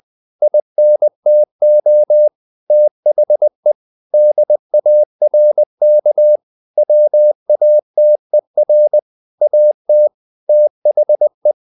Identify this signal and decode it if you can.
Morse code, heard as a keyed tone, about 20 wpm, 600 Hz. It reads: INTO THE DARK WATER AT THE
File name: the old man and the sea / the old man and the sea part 1 20wpm